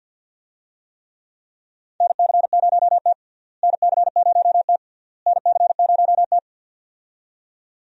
Morse code, heard as a keyed tone, about 47 wpm, 700 Hz. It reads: NC0T NC0T NC0T
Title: USA Call Signs - 4 Characters - Repeated 3x - 50wpm